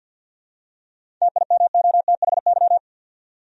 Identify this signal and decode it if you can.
Morse code, heard as a keyed tone, about 52 wpm, 700 Hz. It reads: TIMOTHY
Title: USA Common Names - 50wpm